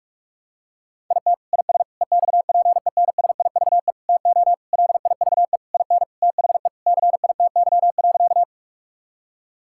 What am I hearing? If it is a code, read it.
Morse code, heard as a keyed tone, about 41 wpm, 700 Hz. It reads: IT IS EXPENSIVE TO LIVE IN THE CITY.
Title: Sentences from Top 1000 Words - Review Part 6 - 45wpm